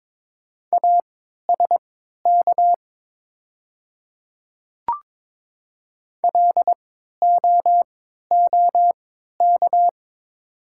Morse code, heard as a keyed tone, about 22 wpm, spaced as for 12 wpm, 700 Hz. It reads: ASK LOOK
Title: 3x Character Spacing - Top 300 Words - 22wpm